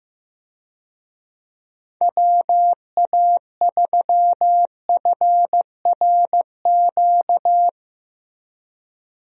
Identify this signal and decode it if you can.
Morse code, heard as a keyed tone, about 15 wpm, 700 Hz. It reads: WA3FRQ
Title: USA Call Signs - 6 Characters - Repeated 2x - 15wpm